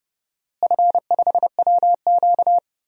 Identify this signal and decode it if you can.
Morse code, heard as a keyed tone, about 30 wpm, 700 Hz. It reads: F5WQ
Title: DXCC-100 Call Signs - 4 Characters - Repeated 2x - 30wpm